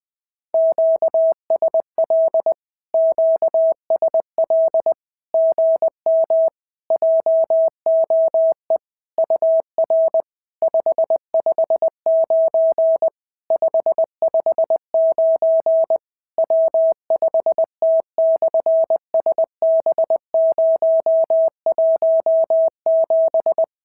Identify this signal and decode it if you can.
Morse code, heard as a keyed tone, about 20 wpm, 650 Hz. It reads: QSL QSL GM JOE UR 559 559 W5T/SB017
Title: SOTA QSO Practice 20wpm